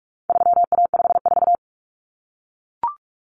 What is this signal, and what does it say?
Morse code, heard as a keyed tone, about 44 wpm, 700 Hz. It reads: 3A54
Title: Common Cut Numbers - 4-Digit 45wpm